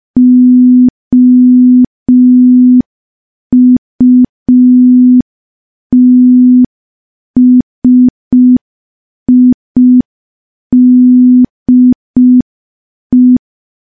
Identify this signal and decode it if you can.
Morse code, heard as a keyed tone, about 5 wpm, 250 Hz. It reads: OUTSIDE
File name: morse / morse-training-05WPM